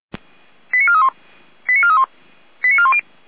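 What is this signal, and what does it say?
Five-tone selective call (ZVEI), heard as quick runs of five tones, tones of about 70 ms. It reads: 89431 89431 89410
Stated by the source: ZVEI five-tone selective call received on an SDR.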